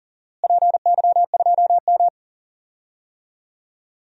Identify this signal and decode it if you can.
Morse code, heard as a keyed tone, about 40 wpm, 700 Hz. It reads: PY2M
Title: DXCC-100 Call Signs - 4 Characters - Repeated 2x - 40wpm